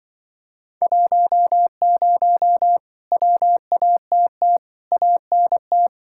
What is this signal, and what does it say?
Morse code, heard as a keyed tone, about 25 wpm, 700 Hz. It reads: 10 WATT ANT
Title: Speed-Racing Common QSO Phrases - 20wpm